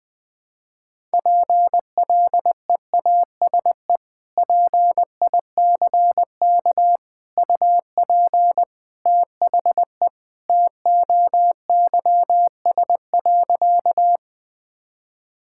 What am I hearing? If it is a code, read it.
Morse code, heard as a keyed tone, about 20 wpm, 700 Hz. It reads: PLEASE PICK UP THE TOYS.
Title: Sentences with 2-Word Phrases - 20wpm